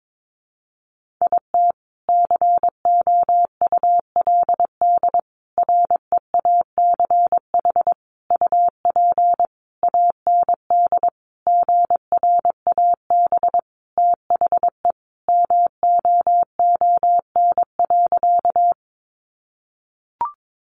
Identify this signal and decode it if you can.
Morse code, heard as a keyed tone, about 22 wpm, 700 Hz. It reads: IT COULD REACH UP AND GRAB THE MOON.
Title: A Sound Of Thunder 22wpm